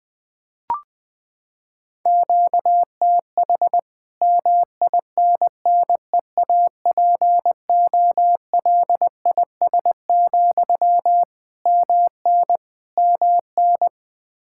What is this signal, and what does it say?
Morse code, heard as a keyed tone, about 20 wpm, 700 Hz. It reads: QTH MINNEAPOLIS, MN MN
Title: Practice QSO - Exchange 1 - Beginning - 20wpm